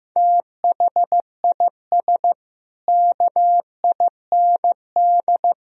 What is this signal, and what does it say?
Morse code, heard as a keyed tone, about 15 wpm, 700 Hz. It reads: THIS KIND
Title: Speed-Racing - Sets of 3 Words 15wpm